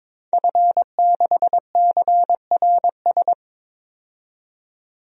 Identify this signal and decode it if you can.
Morse code, heard as a keyed tone, about 22 wpm, 700 Hz. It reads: F6CRS
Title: DXCC-100 Call Signs - 5 Characters - 22wpm